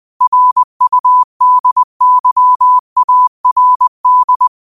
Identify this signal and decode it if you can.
Morse code, heard as a keyed tone, about 20 wpm, 1000 Hz. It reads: RUDYARD
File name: qotdfiles / 20wpm